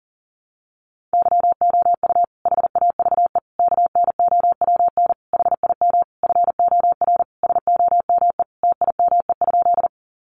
Morse code, heard as a keyed tone, about 40 wpm, 700 Hz. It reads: YOU HAVE KNOWN HIM FOR SOME TIME?
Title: Sentences from Top 600 Words 40wpm